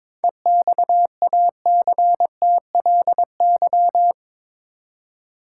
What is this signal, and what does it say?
Morse code, heard as a keyed tone, about 22 wpm, 700 Hz. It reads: EXACTLY
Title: Seven Letter Words - 22wpm